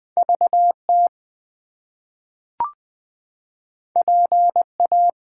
Morse code, heard as a keyed tone, about 20 wpm, 700 Hz. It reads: VT PA
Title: US States and Canadian Provinces 20wpm